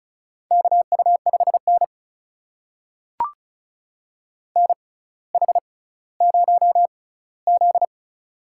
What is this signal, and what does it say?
Morse code, heard as a keyed tone, about 36 wpm, spaced as for 18 wpm, 700 Hz. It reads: KU5N NH0Z
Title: USA Call Signs - 4 Characters - 6x Character Spacing - 35wpm